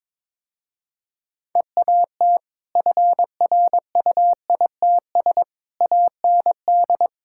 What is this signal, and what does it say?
Morse code, heard as a keyed tone, about 21 wpm, 700 Hz. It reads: EAT FRUITS AND
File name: practice / 22wpm / Easy Sentences with Hints - 22wpm